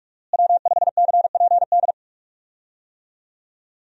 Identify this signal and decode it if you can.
Morse code, heard as a keyed tone, about 46 wpm, 700 Hz. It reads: W5CPD
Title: USA Call Signs - 5 Characters - Repeated 3x - 45wpm